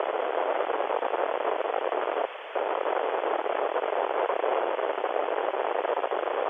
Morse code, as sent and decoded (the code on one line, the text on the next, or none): none